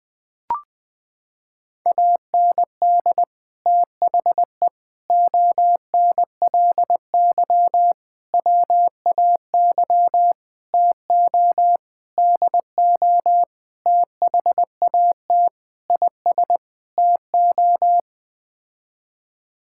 Morse code, heard as a keyed tone, about 20 wpm, 700 Hz. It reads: AND THE ONLY WAY TO DO THAT IS TO
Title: Sets of 9 Words - 20wpm